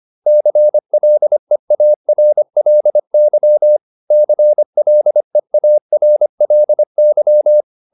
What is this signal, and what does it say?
Morse code, heard as a keyed tone, about 25 wpm, 600 Hz. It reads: CLEARLY CLEARLY